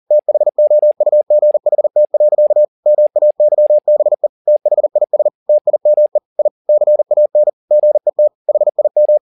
Morse code, heard as a keyed tone, about 40 wpm, 600 Hz. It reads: THOUGHT. MAYBE THIS TIME I CAN GET HIM
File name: the old man and the sea 2 40wpm